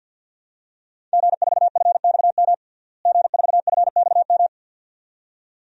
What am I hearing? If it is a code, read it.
Morse code, heard as a keyed tone, about 51 wpm, 700 Hz. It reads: G4FXM G4FXM
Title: DXCC-100 Call Signs - 5 Characters - Repeated 2x - 50wpm